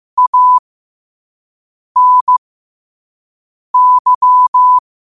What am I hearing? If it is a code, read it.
Morse code, heard as a keyed tone, about 15 wpm, spaced as for 5 wpm, 1000 Hz. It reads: ANY
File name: qotdfiles / 5wpm